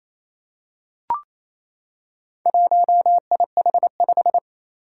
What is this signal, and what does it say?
Morse code, heard as a keyed tone, about 28 wpm, 700 Hz. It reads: 1IH5
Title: Four Challenging Letter-Number Combinations 28wpm